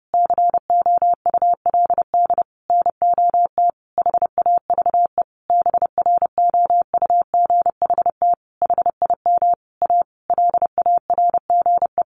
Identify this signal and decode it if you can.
Morse code, heard as a keyed tone, about 30 wpm, 700 Hz. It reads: COULD NOT HAVE BROUGHT HIM A LARGE
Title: Sentences from Top 1100 Words 30wpm